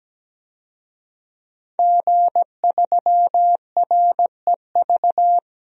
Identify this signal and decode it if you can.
Morse code, heard as a keyed tone, about 17 wpm, 700 Hz. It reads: G3REV
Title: DXCC-100 Call Signs - 5 Characters - Repeated 3x - 17wpm